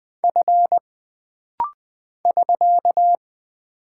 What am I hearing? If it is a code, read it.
Morse code, heard as a keyed tone, about 20 wpm, 700 Hz. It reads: F <SK>
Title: Single Character (Letter:Number:Punctuation:Prosign) - Rapid-Fire - 20wpm